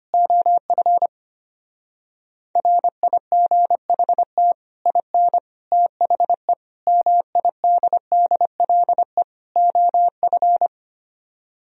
Morse code, heard as a keyed tone, about 25 wpm, 700 Hz. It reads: OF RIGHT IN THE MIDDLE OF
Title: Speed-Racing - Sets of 5 Words 25wpm